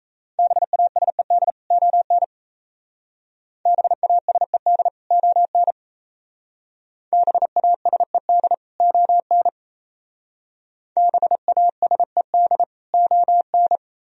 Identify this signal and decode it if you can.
Morse code, heard as a keyed tone, about 34 wpm, 700 Hz. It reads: BASED ON BASED ON BASED ON BASED ON
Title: Speed-Racing - Sets of 2 Words 28wpm